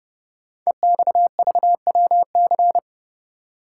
Morse code, heard as a keyed tone, about 30 wpm, 700 Hz. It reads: EXVWC
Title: Code Groups - 5 Letters - 30wpm